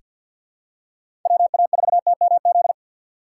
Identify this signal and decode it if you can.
Morse code, heard as a keyed tone, about 47 wpm, 700 Hz. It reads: WA4TMZ
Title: USA Call Signs - 6 Characters - 6x Character Spacing - 50wpm